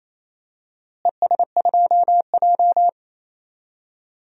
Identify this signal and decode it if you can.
Morse code, heard as a keyed tone, about 28 wpm, 700 Hz. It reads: ES2J
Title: DXCC-100 Call Signs - 4 Characters - Repeated 2x - 28wpm